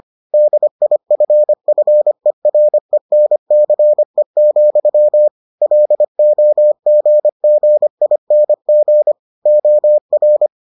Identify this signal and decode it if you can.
Morse code, heard as a keyed tone, about 25 wpm, 600 Hz. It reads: DIFFERENCE, LOGGING OR